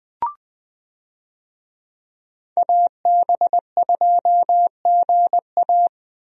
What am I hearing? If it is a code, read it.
Morse code, heard as a keyed tone, about 20 wpm, 700 Hz. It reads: AB2GA
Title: USA Call Signs - 5 Characters - 20wpm